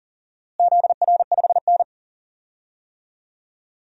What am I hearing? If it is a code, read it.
Morse code, heard as a keyed tone, about 40 wpm, 700 Hz. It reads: ZR5N
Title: DXCC-100 Call Signs - 4 Characters - Repeated 3x - 40wpm